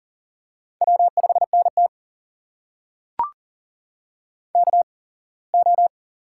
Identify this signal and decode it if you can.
Morse code, heard as a keyed tone, about 38 wpm, spaced as for 23 wpm, 700 Hz. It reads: W5NT KO
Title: USA Call Signs - 4 Characters - 8x Character Spacing - 40wpm